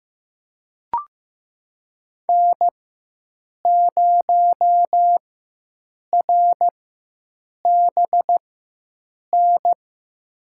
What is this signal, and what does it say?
Morse code, heard as a keyed tone, about 15 wpm, spaced as for 7 wpm, 700 Hz. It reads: N0RBN
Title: USA Call Signs - 5 Characters - 4x Character Spacing - 15wpm